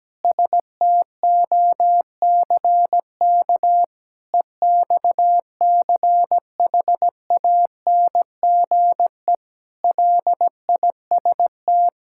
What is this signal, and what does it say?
Morse code, heard as a keyed tone, about 17 wpm, 700 Hz. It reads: STOCK EXCHANGE LIST
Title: Sentences from Top 1000 Words - Review Part 6 - 17wpm